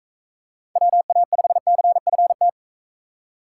Morse code, heard as a keyed tone, about 42 wpm, 700 Hz. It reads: WA5CFT